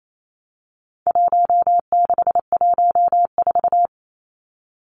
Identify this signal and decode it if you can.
Morse code, heard as a keyed tone, about 28 wpm, 700 Hz. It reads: 1614
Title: Numbers - Contest Style - 4-Digit 28wpm